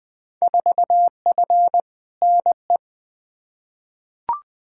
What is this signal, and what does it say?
Morse code, heard as a keyed tone, about 20 wpm, 700 Hz. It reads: 4F NE
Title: Field Day Exchanges Encore - 20wpm